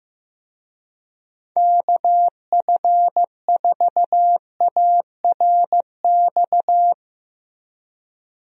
Morse code, heard as a keyed tone, about 15 wpm, 700 Hz. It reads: KF4ARX
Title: USA Call Signs - 6 Characters - Repeated 2x - 15wpm